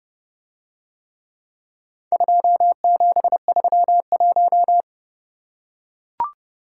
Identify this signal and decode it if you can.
Morse code, heard as a keyed tone, about 31 wpm, 700 Hz. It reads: 2731